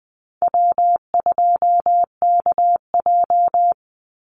Morse code, heard as a keyed tone, about 19 wpm, 700 Hz. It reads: W2KJ
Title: Call Signs - 20wpm